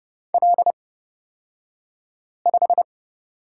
Morse code, heard as a keyed tone, about 30 wpm, spaced as for 5 wpm, 700 Hz. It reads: L5
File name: Lesson 36d - BT - Two Characters 30wpm